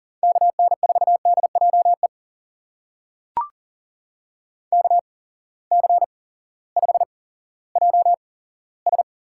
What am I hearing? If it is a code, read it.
Morse code, heard as a keyed tone, about 42 wpm, spaced as for 18 wpm, 700 Hz. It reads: KN4DJE KC5JS
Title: USA Call Signs - 6 Characters - 8x Character Spacing - 40wpm